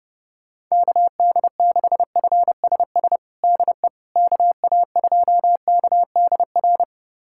Morse code, heard as a keyed tone, about 30 wpm, 700 Hz. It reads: KD6FSS DE KA2KDR
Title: USA Callsign DE Callsign 30wpm